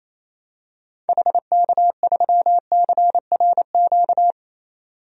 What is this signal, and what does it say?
Morse code, heard as a keyed tone, about 28 wpm, 700 Hz. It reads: HK3CRQ
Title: DXCC-100 Call Signs - 6 Characters - 28wpm